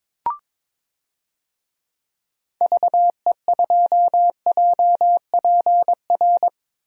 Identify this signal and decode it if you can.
Morse code, heard as a keyed tone, about 22 wpm, 700 Hz. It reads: VE2JPR
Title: DXCC-100 Call Signs - 6 Characters - 22wpm